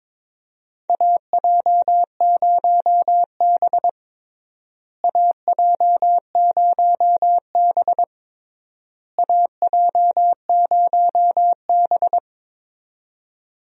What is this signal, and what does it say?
Morse code, heard as a keyed tone, about 22 wpm, 700 Hz. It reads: AJ0B AJ0B AJ0B
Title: USA Call Signs - 4 Characters - Repeated 3x - 22wpm